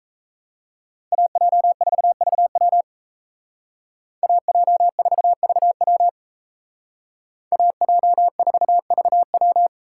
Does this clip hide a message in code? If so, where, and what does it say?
Yes: Morse code from 1.1 to 9.7 s, heard as a keyed tone, about 35 wpm, 700 Hz. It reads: AJ4VW AJ4VW AJ4VW